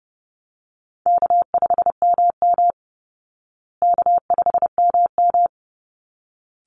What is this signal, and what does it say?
Morse code, heard as a keyed tone, about 30 wpm, 700 Hz. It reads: K5MM K5MM
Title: Easy Call Signs Repeated 2x - 30wpm